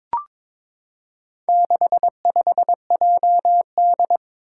Morse code, heard as a keyed tone, about 22 wpm, 700 Hz. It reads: 65JD
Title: Four Challenging Letter-Number Combinations 22wpm